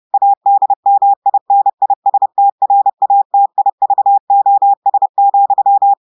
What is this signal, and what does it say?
Morse code, heard as a keyed tone, about 30 wpm, 800 Hz. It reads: ADMINISTRATIVOS,
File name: audios / portugais / portugais texte en clair 2 vitesse 30 mots minutes